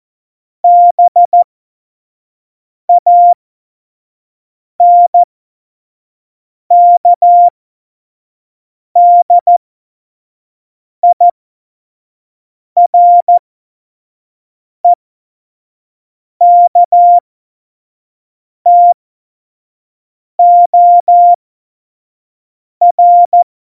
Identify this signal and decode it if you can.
Morse code, heard as a keyed tone, about 14 wpm, spaced as for 5 wpm, 700 Hz. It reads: BANKDIREKTOR
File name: Morsekurs / Witze / Witz 22. Dezember 2020, 25 F70